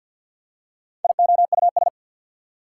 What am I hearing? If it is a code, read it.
Morse code, heard as a keyed tone, about 49 wpm, 690 Hz. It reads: IOUS
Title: Prefixes and Suffixes 50wpm